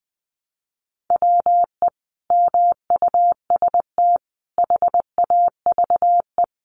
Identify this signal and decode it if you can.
Morse code, heard as a keyed tone, about 20 wpm, 700 Hz. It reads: WE MUST HAVE
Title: Sentences from Top 1200 Words 20wpm